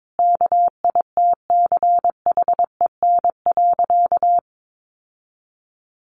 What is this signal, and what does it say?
Morse code, heard as a keyed tone, about 22 wpm, 700 Hz. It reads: KITCHEN.
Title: Sentences from Top 1400 Words 22wpm